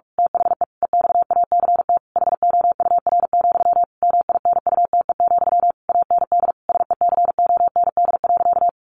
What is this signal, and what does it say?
Morse code, heard as a keyed tone, about 44 wpm, 700 Hz. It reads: THE EXACT HOUR, MINUTE, AND SECOND.
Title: A Sound Of Thunder 45wpm